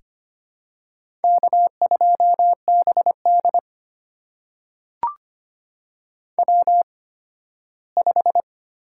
Morse code, heard as a keyed tone, about 25 wpm, spaced as for 15 wpm, 700 Hz. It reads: K2BD W5
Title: USA Call Signs - 4 Characters - 8x Character Spacing - 25wpm